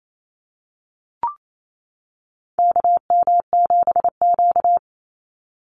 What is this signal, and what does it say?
Morse code, heard as a keyed tone, about 28 wpm, 700 Hz. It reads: KM7Q
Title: Call Signs - 28wpm